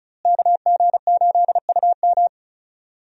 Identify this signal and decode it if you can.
Morse code, heard as a keyed tone, about 36 wpm, 700 Hz. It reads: KG8UM
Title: USA Call Signs - 5 Characters - Repeated 2x - 35wpm